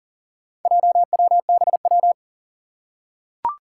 Morse code, heard as a keyed tone, about 40 wpm, 700 Hz. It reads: JWBW
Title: Four Challenging Letter-Number Combinations 40wpm